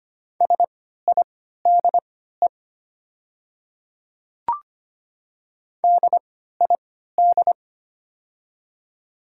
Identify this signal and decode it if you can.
Morse code, heard as a keyed tone, about 25 wpm, spaced as for 14 wpm, 690 Hz. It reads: SIDE DID
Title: 3x Character Spacing - Top 100 Words - 25wpm